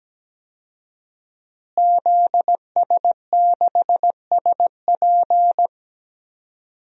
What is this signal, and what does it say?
Morse code, heard as a keyed tone, about 17 wpm, 700 Hz. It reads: ZS6SP
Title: DXCC-100 Call Signs - 5 Characters - Repeated 3x - 17wpm